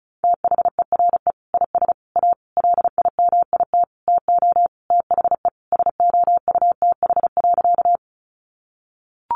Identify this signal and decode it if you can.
Morse code, heard as a keyed tone, about 34 wpm, 700 Hz. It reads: THERE IS A LIMIT TO THE SOUTH.
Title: Sentences from Top 1500 Words 35wpm